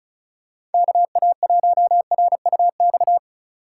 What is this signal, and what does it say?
Morse code, heard as a keyed tone, about 36 wpm, 700 Hz. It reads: KA1RUX
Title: USA Call Signs - 6 Characters - 3x Character Spacing - 35wpm